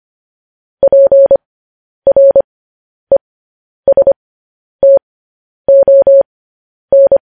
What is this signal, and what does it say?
Morse code, heard as a keyed tone, about 25 wpm, spaced as for 10 wpm, 550 Hz. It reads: PRESTON
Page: news-headlines-in-morse-code-at-10-wpm